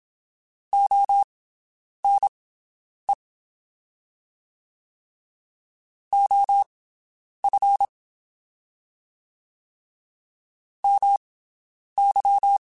Morse code, heard as a keyed tone, about 27 wpm, spaced as for 9 wpm, 800 Hz. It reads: ONE OF MY
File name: QSOs Spaced